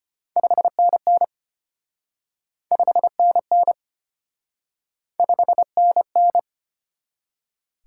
Morse code, heard as a keyed tone, about 31 wpm, 700 Hz. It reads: 5NN 5NN 5NN
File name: practice / 25wpm / Speed-Racing Top 20 QSO Elements - 25wpm